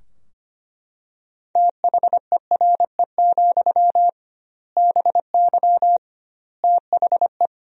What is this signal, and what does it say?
Morse code, heard as a keyed tone, about 25 wpm, 700 Hz. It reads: THERE, BY THE
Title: 2x Word-Spacing - Sentences from Top 500 Words - 25wpm